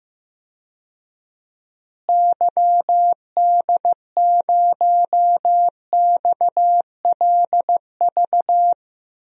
Morse code, heard as a keyed tone, about 15 wpm, 700 Hz. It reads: YD0XLV
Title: DXCC-100 Call Signs - 6 Characters - Repeated 3x - 15wpm